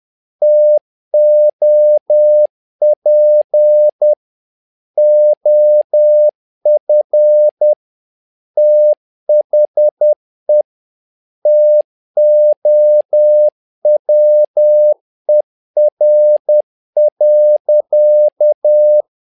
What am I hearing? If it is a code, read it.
Morse code, heard as a keyed tone, about 10 wpm, 600 Hz. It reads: TOP OF THE TOWER.
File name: learning center / tower trouble / tower trouble 10wpm